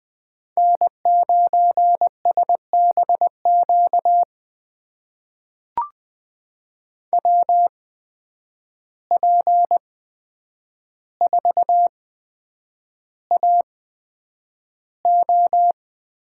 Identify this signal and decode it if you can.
Morse code, heard as a keyed tone, about 20 wpm, spaced as for 9 wpm, 700 Hz. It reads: N9SBQ WP4AO
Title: USA Call Signs - 5 Characters - 8x Character Spacing - 20wpm